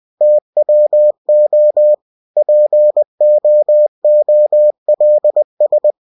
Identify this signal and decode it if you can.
Morse code, heard as a keyed tone, about 20 wpm, 600 Hz. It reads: TWO POOLS
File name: winnie-20wpm